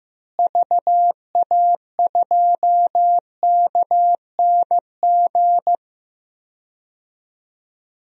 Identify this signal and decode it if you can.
Morse code, heard as a keyed tone, about 15 wpm, 700 Hz. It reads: VA2KNG
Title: DXCC-100 Call Signs - 6 Characters - Repeated 2x - 15wpm